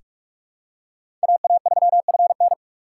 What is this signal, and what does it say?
Morse code, heard as a keyed tone, about 42 wpm, 700 Hz. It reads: AA3FN
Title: USA Call Signs - 4x Character Spacing - 45wpm